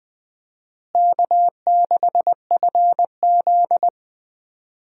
Morse code, heard as a keyed tone, about 20 wpm, 700 Hz. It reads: K6FZ
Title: USA Call Signs - 4 Characters - Repeated 2x - 20wpm